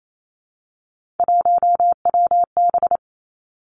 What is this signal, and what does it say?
Morse code, heard as a keyed tone, about 28 wpm, 700 Hz. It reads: 1WB
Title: Three Challenging Letter-Number Combinations 28wpm